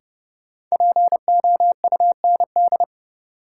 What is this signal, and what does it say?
Morse code, heard as a keyed tone, about 30 wpm, 700 Hz. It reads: POUND
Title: Sentences from Top 1000 Words - Review Part 2 - New Word Drill - 30wpm